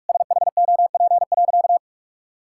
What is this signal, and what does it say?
Morse code, heard as a keyed tone, about 46 wpm, 700 Hz. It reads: SHOP.
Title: Sentences from Top 1600 Words 45wpm